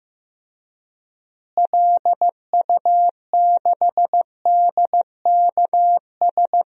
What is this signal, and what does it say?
Morse code, heard as a keyed tone, about 15 wpm, 700 Hz. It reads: LU6DKS
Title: DXCC-100 Call Signs - 6 Characters - Repeated 3x - 15wpm